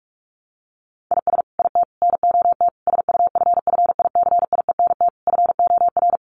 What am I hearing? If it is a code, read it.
Morse code, heard as a keyed tone, about 44 wpm, 700 Hz. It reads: IS IT NOT SUFFICIENT FOR
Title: Sentences from Top 1400 Words 45wpm